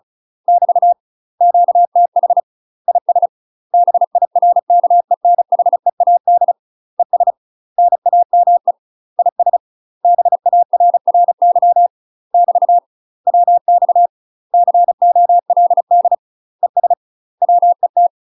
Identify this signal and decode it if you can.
Morse code, heard as a keyed tone, about 36 wpm, 700 Hz. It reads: <BT> QTH IS BIRKENHEAD ES NAME IS BARRY <BT> WX COLD ES WET
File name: UK MASTER QSOs TRAINING / 35WPM / UK MASTER QSOs 0002